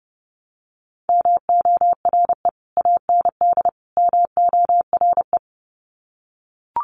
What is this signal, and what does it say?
Morse code, heard as a keyed tone, about 30 wpm, 700 Hz. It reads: MORE AND MORE
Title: Sets of 3 Words 30wpm